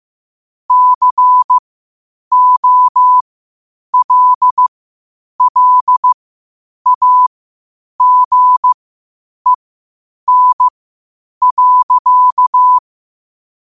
Morse code, heard as a keyed tone, about 15 wpm, spaced as for 8 wpm, 1000 Hz. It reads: COLLAGEN.
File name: qotdfiles / 8wpm